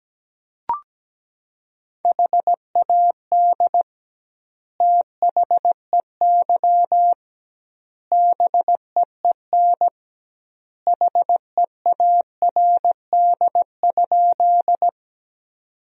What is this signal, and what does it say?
Morse code, heard as a keyed tone, about 17 wpm, 700 Hz. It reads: HAD THEY BEEN HEARD?
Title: 2x Word-Spacing - Sentences from Top 400 Words - 17wpm